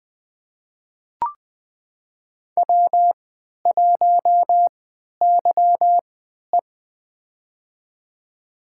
Morse code, heard as a keyed tone, about 20 wpm, spaced as for 11 wpm, 700 Hz. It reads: W1YE
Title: USA Call Signs - 4 Characters - 3x Character Spacing - 20wpm